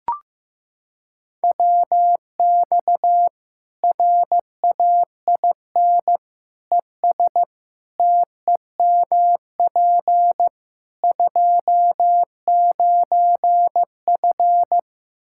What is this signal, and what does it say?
Morse code, heard as a keyed tone, about 15 wpm, 700 Hz. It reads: WX RAIN ES TEMP 29F
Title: Practice QSO - Exchange 2 - Info - 15wpm